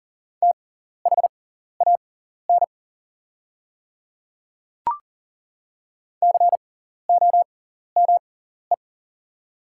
Morse code, heard as a keyed tone, about 40 wpm, spaced as for 14 wpm, 700 Hz. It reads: THAN COME
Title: 6x Character Spacing - Top 100 Words - 40wpm